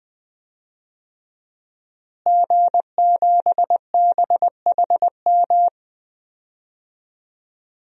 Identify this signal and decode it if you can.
Morse code, heard as a keyed tone, about 20 wpm, 700 Hz. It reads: G7BHM
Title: DXCC-100 Call Signs - 5 Characters - Repeated 3x - 20wpm